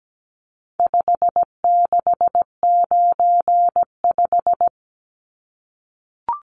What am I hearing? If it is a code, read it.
Morse code, heard as a keyed tone, about 17 wpm, 700 Hz. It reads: 5695
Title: Numbers - 4-Digit 17wpm